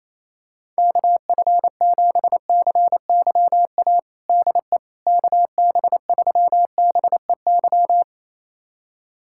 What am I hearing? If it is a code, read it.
Morse code, heard as a keyed tone, about 28 wpm, 700 Hz. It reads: KF7CYA DE KB3BEY
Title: USA Callsign DE Callsign - 6 Characters - 28wpm